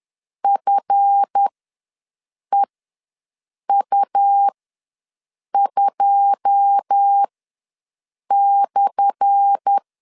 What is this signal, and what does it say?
Morse code, heard as a keyed tone, about 10 wpm, spaced as for 6 wpm, 800 Hz. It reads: FEU2/
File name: LEZIONI / 10 SIGNS / 10A SLASH ws 10wpm 4121